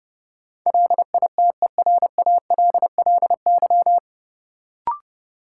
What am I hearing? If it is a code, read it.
Morse code, heard as a keyed tone, about 30 wpm, 700 Hz. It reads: LITERALLY
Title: Sentences from Top 1400 Words - New Word Drill - 30wpm